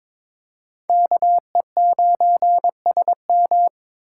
Morse code, heard as a keyed tone, about 26 wpm, 700 Hz. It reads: KE9SM
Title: Speed Racing - USA 2x2 Call Signs 22wpm